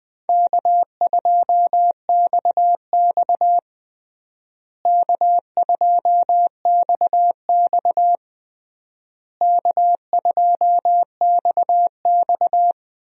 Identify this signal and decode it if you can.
Morse code, heard as a keyed tone, about 20 wpm, 690 Hz. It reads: K2XX K2XX K2XX
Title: USA Call Signs - 4 Characters - Repeated 3x - 20wpm